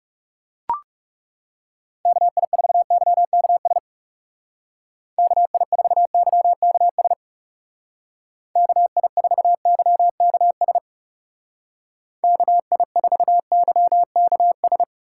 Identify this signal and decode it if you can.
Morse code, heard as a keyed tone, about 37 wpm, 700 Hz. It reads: KI4YKS KI4YKS KI4YKS KI4YKS
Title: Speed Racing - USA 2x3 Call Signs 30wpm